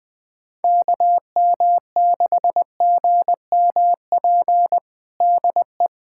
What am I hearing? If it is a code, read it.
Morse code, heard as a keyed tone, about 20 wpm, 700 Hz. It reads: KM6GMP DE
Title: USA Callsign DE Callsign - 6 Characters - 20wpm